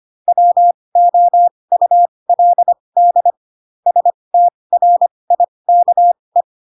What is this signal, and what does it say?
Morse code, heard as a keyed tone, about 25 wpm, spaced as for 20 wpm, 700 Hz. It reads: WOULD STRIKE
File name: Pinocchio / Pinocchio ch4 15 25 ik1hge